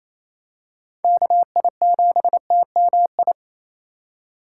Morse code, heard as a keyed tone, about 28 wpm, 700 Hz. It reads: KI7TMI
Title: USA Call Signs - 2x Character Spacing - 28wpm